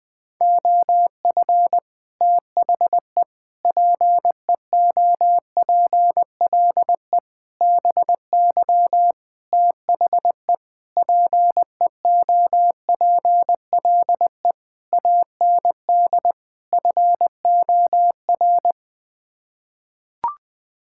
Morse code, heard as a keyed tone, about 20 wpm, 700 Hz. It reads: OF THE PEOPLE BY THE PEOPLE AND FOR
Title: Sets of 8 Words - 20wpm